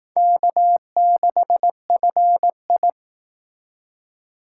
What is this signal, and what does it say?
Morse code, heard as a keyed tone, about 18 wpm, 700 Hz. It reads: K6FI
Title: Speed Racing - USA 1x2 Call Signs 15wpm